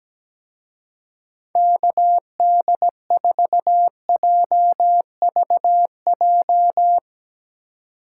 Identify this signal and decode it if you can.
Morse code, heard as a keyed tone, about 17 wpm, 700 Hz. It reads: KD4JVJ
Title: Speed Racing - USA 2x3 Call Signs 17wpm